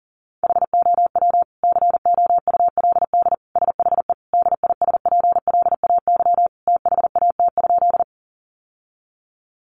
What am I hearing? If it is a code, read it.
Morse code, heard as a keyed tone, about 37 wpm, 700 Hz. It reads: HOW COULD SHE DISPLAY THAT?
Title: Sentences from Top 1200 Words 40wpm